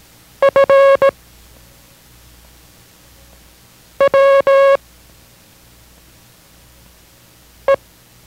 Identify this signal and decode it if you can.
Morse code, heard as a keyed tone, about 16 wpm, 550 Hz. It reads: F W E